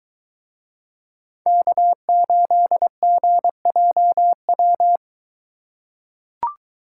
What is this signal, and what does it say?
Morse code, heard as a keyed tone, about 23 wpm, 700 Hz. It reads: K8GJW